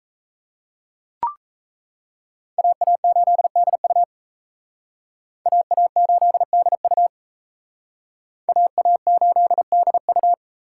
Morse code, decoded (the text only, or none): AA8DU AA8DU AA8DU